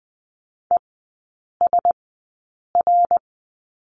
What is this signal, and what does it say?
Morse code, heard as a keyed tone, about 20 wpm, spaced as for 8 wpm, 690 Hz. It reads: ESR
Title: 093 Lesson 16e - Letter c - Three Characters 20wpm